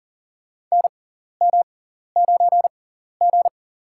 Morse code, heard as a keyed tone, about 37 wpm, spaced as for 13 wpm, 700 Hz. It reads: NM9G
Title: USA Call Signs - 6x Character Spacing - 40wpm